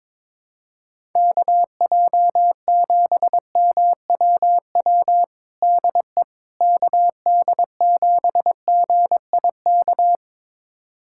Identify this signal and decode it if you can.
Morse code, heard as a keyed tone, about 22 wpm, 700 Hz. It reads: KJ7MWW DE KD7GIK
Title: USA Callsign DE Callsign - 6 Characters - 22wpm